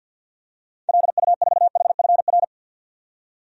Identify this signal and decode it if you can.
Morse code, heard as a keyed tone, about 51 wpm, 700 Hz. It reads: LU4HFR
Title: DXCC-100 Call Signs - 6 Characters - 50wpm